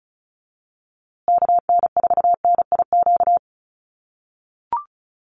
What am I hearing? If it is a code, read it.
Morse code, heard as a keyed tone, about 34 wpm, 700 Hz. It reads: KN4NIQ
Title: USA Call Signs Repeated 3x 35wpm